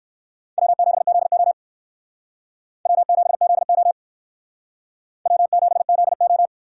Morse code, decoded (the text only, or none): W7ZO W7ZO W7ZO